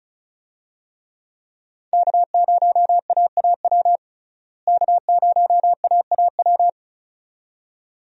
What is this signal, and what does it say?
Morse code, heard as a keyed tone, about 36 wpm, 700 Hz. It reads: K0AAW K0AAW
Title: USA Call Signs - 5 Characters - Repeated 2x - 35wpm